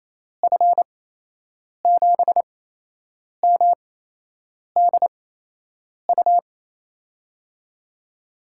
Morse code, heard as a keyed tone, about 28 wpm, spaced as for 8 wpm, 700 Hz. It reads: F7MDU
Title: USA Call Signs - 6 Characters - 8x Character Spacing - 28wpm